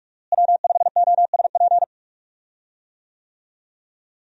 Morse code, heard as a keyed tone, about 46 wpm, 700 Hz. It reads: W5OSP
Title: USA Call Signs - 5 Characters - Repeated 3x - 45wpm